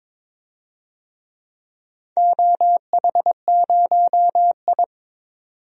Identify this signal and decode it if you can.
Morse code, heard as a keyed tone, about 22 wpm, 700 Hz. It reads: OH0I
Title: DXCC-100 Call Signs - 4 Characters - 22wpm